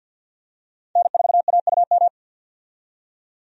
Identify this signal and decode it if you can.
Morse code, heard as a keyed tone, about 52 wpm, 700 Hz. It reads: N4AUM
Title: USA Call Signs - 5 Characters - 50wpm